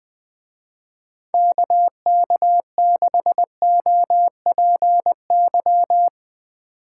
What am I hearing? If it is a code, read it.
Morse code, heard as a keyed tone, about 19 wpm, 700 Hz. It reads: KK6OPY
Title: Speed Racing - USA 2x3 Call Signs 20wpm